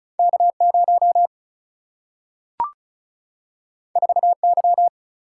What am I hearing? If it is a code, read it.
Morse code, heard as a keyed tone, about 36 wpm, 700 Hz. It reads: K0 4Y
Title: Single-Letter Single-Number 35wpm